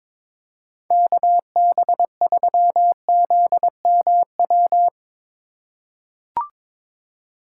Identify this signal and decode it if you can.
Morse code, heard as a keyed tone, about 22 wpm, 700 Hz. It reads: KB3ZMW
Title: USA Call Signs - 6 Characters - 8x Character Spacing - 22wpm